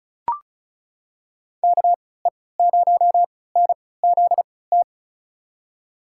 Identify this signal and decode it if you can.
Morse code, heard as a keyed tone, about 36 wpm, spaced as for 20 wpm, 700 Hz. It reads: KE0NZT
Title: USA Call Signs - 6 Characters - 3x Character Spacing - 35wpm